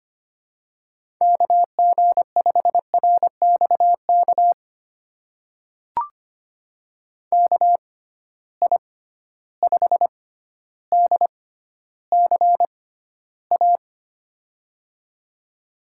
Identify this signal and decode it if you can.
Morse code, heard as a keyed tone, about 25 wpm, spaced as for 13 wpm, 700 Hz. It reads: KG5RXK KI5DCA
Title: USA Call Signs - 6 Characters - 6x Character Spacing - 25wpm